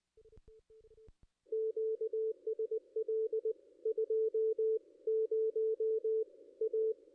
Morse code, heard as a keyed tone, about 20 wpm, 440 Hz, with radio noise. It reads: QSL 20 A
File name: contests / ss-cw-k5tr-op-n5rz -2200